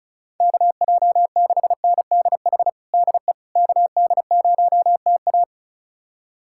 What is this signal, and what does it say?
Morse code, heard as a keyed tone, about 34 wpm, 700 Hz. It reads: KJ6NDH DE KD0TA
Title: USA Callsign DE Callsign 35wpm